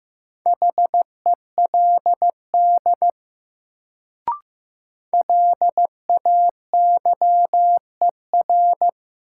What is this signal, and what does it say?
Morse code, heard as a keyed tone, about 15 wpm, 700 Hz. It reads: HELD LAYER
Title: Sentences from Top 1600 Words - New Word Drill Alt - 15wpm